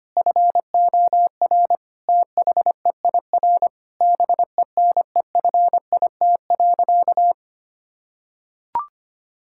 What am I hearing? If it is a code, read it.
Morse code, heard as a keyed tone, about 25 wpm, 700 Hz. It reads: FOR THEIR BENEFIT.
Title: Sentences from Top 1000 Words - Review Part 6 - 25wpm